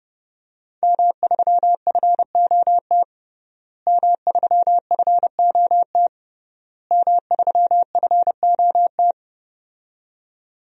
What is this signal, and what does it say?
Morse code, heard as a keyed tone, about 30 wpm, 700 Hz. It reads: M3FOT M3FOT M3FOT
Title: DXCC-100 Call Signs - 5 Characters - Repeated 3x - 30wpm